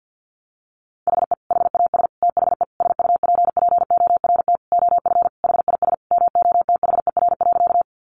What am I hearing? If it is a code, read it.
Morse code, heard as a keyed tone, about 50 wpm, 700 Hz. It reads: HE HAS THE SUPPORT OF HIS MOTHER.
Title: Sentences from Top 800 Words 50wpm